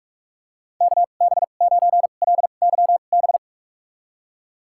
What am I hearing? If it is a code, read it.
Morse code, heard as a keyed tone, about 46 wpm, spaced as for 32 wpm, 700 Hz. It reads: KB9LYB
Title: USA Call Signs - 6 Characters - 2x Character Spacing - 45wpm